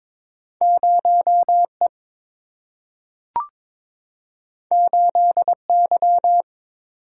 Morse code, heard as a keyed tone, about 22 wpm, 700 Hz. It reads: 0 E 8 Y
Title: Single-Letter Single-Number 22wpm